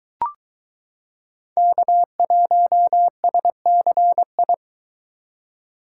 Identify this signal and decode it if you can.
Morse code, heard as a keyed tone, about 22 wpm, 700 Hz. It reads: K1SCI